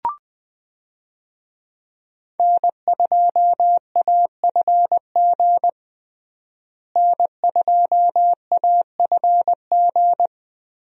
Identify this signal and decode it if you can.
Morse code, heard as a keyed tone, about 21 wpm, 700 Hz. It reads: N2AFG N2AFG
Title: USA Call Signs - 5 Characters - Repeated 3x - 20wpm